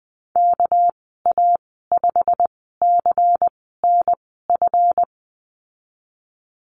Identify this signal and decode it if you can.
Morse code, heard as a keyed tone, about 20 wpm, spaced as for 15 wpm, 700 Hz. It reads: KA5CNF
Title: 207 Lesson 33g - Letter j - Call Signs 20wpm